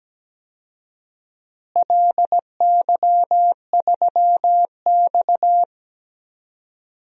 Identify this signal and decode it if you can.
Morse code, heard as a keyed tone, about 17 wpm, 700 Hz. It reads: LY3X
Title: DXCC-100 Call Signs - 4 Characters - Repeated 3x - 17wpm